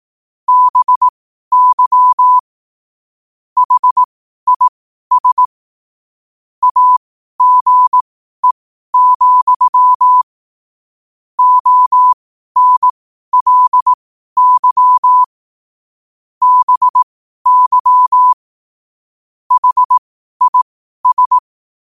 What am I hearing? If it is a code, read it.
Morse code, heard as a keyed tone, about 18 wpm, spaced as for 12 wpm, 1000 Hz. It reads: BY HIS AGE, ONLY BY HIS
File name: qotdfiles / 12wpm